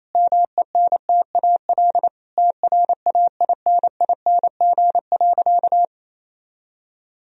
Morse code, heard as a keyed tone, about 28 wpm, 700 Hz. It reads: MENTAL TRAINING.
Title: Sentences from Top 1000 Words - Review Part 4 - 28wpm